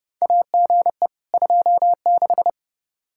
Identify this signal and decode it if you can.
Morse code, heard as a keyed tone, about 32 wpm, 700 Hz. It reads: AGE 26
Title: Practice QSO - Exchange 3 - More Info - 30wpm